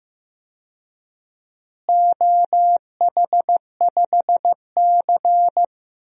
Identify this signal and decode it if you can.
Morse code, heard as a keyed tone, about 15 wpm, 700 Hz. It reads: OH5C
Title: DXCC-100 Call Signs - 4 Characters - Repeated 3x - 15wpm